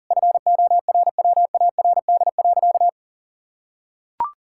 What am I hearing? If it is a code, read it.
Morse code, heard as a keyed tone, about 40 wpm, 700 Hz. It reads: FORWARD.
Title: Pangram Challenge v2 - 40wpm